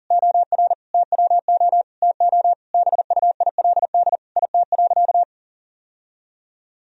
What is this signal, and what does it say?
Morse code, heard as a keyed tone, about 40 wpm, 700 Hz. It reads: OR TWO TO BUILD IT.
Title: Sentences from Top 1000 Words - Review Part 1 - 40wpm